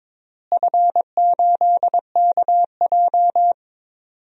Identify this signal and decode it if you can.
Morse code, heard as a keyed tone, about 22 wpm, 700 Hz. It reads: F8KJ
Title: DXCC-100 Call Signs - 4 Characters - Repeated 2x - 22wpm